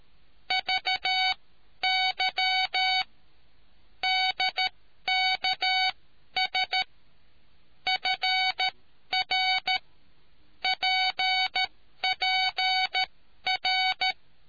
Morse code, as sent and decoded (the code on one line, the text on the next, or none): ...- -.-- / -.. -.- ... / ..-. .-. / .--. .--. .-.
VY DKS FR PPR